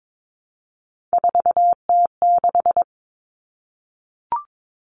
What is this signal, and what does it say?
Morse code, heard as a keyed tone, about 22 wpm, 700 Hz. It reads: 4T6
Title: All Cut Numbers - 3-Digit 22wpm